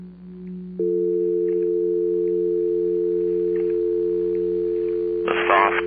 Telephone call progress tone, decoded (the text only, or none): dial tone